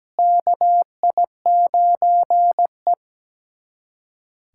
Morse code, heard as a keyed tone, about 17 wpm, 700 Hz. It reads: KI9E
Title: USA Callsign DE Callsign - 4 Characters - 17wpm